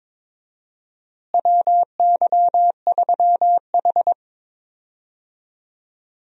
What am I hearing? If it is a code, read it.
Morse code, heard as a keyed tone, about 22 wpm, 700 Hz. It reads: WY3H
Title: USA Call Signs - 4 Characters - Repeated 2x - 22wpm